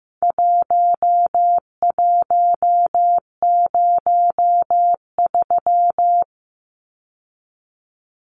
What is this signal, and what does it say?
Morse code, heard as a keyed tone, about 15 wpm, 700 Hz. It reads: 1103
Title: Numbers - Contest Style - 4-Digit 15wpm